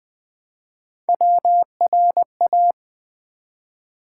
Morse code, heard as a keyed tone, about 20 wpm, 700 Hz. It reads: WRA
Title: Build A Word - 1 Letter At A Time - 4-Letter Words - 20wpm